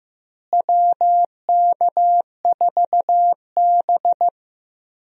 Morse code, heard as a keyed tone, about 15 wpm, 700 Hz. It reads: WK4B
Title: USA Call Signs - 4 Characters - Repeated 3x - 15wpm